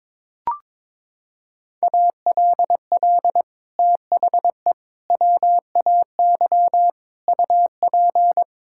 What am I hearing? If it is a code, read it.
Morse code, heard as a keyed tone, about 22 wpm, 700 Hz. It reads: ALL THE WAY UP
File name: practice / 22wpm / Sets of 6 Words - 22wpm